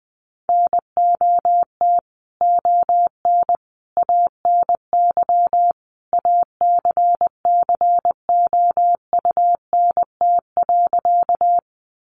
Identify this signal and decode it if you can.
Morse code, heard as a keyed tone, about 20 wpm, 700 Hz. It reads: NOT ON ANY ACCOUNT.
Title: Sentences from Top 900 Words 20wpm